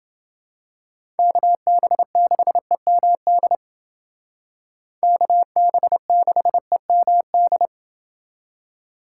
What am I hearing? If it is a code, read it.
Morse code, heard as a keyed tone, about 29 wpm, 700 Hz. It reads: KB6EMD KB6EMD